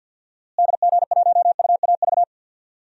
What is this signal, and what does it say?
Morse code, heard as a keyed tone, about 48 wpm, 700 Hz. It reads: DG1UAV
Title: DXCC-100 Call Signs - 6 Characters - Repeated 2x - 50wpm